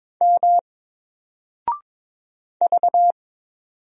Morse code, heard as a keyed tone, about 22 wpm, 700 Hz. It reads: M V
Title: Single Letter-Number Alt 22wpm